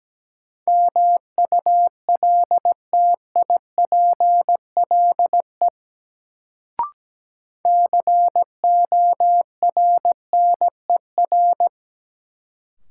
Morse code, heard as a keyed tone, about 17 wpm, 700 Hz. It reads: MULTIPLE CORNER
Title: Sets of 5 Words - New Word Drill - 17wpm